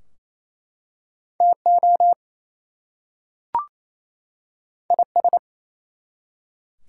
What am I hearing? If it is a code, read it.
Morse code, heard as a keyed tone, about 27 wpm, 700 Hz. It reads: TO IS
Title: Two Letter Words - 28wpm